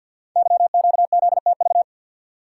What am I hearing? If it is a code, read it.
Morse code, heard as a keyed tone, about 52 wpm, 700 Hz. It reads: YQZTV
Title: Code Groups - 5 Letter:Numbers - 50wpm